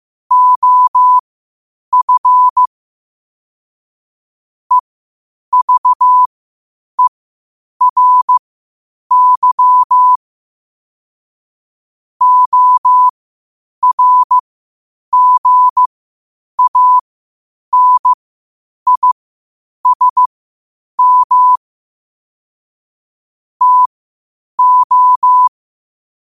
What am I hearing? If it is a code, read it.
Morse code, heard as a keyed tone, about 15 wpm, spaced as for 8 wpm, 1000 Hz. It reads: OF EVERY ORGANISM TO
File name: qotdfiles / 8wpm